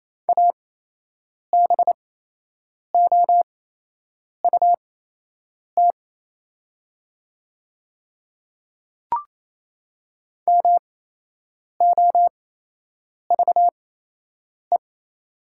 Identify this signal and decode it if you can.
Morse code, heard as a keyed tone, about 28 wpm, spaced as for 8 wpm, 700 Hz. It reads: ABOUT MOVE
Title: 8x Character Spacing - Top 300 Words - 28wpm